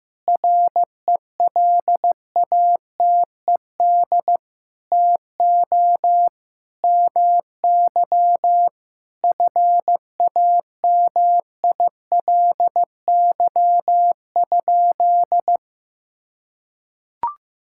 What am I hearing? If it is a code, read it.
Morse code, heard as a keyed tone, about 15 wpm, 700 Hz. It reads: RELATED TO MY FAMILY?
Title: Sentences from Top 1000 Words - Review Part 5 - 15wpm